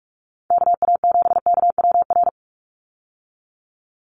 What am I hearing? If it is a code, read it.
Morse code, heard as a keyed tone, about 46 wpm, 690 Hz. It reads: KA7KWR
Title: USA Call Signs 45wpm